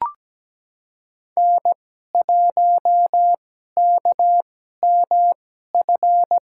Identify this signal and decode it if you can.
Morse code, heard as a keyed tone, about 17 wpm, spaced as for 12 wpm, 700 Hz. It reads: N1KMF
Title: USA Call Signs - 2x Character Spacing - 17wpm